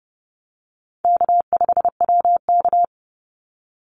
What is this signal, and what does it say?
Morse code, heard as a keyed tone, about 30 wpm, 700 Hz. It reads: K5WK
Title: Easy Call Signs Repeated 3x - 30wpm